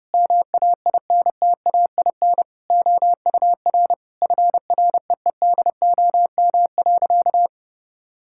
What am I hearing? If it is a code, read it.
Morse code, heard as a keyed tone, about 30 wpm, 700 Hz. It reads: MAINTAIN OUR FREEDOM.
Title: Sentences from Top 1000 Words - Review Part 5 - 30wpm